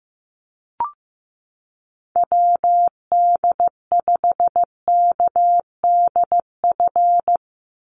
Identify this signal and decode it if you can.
Morse code, heard as a keyed tone, about 15 wpm, 700 Hz. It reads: WD5KDF
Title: USA Call Signs 15wpm